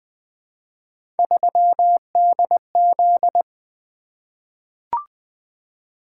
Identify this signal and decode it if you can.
Morse code, heard as a keyed tone, about 20 wpm, 700 Hz. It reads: 3DZ
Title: Three Letter-Number Combinations - 20wpm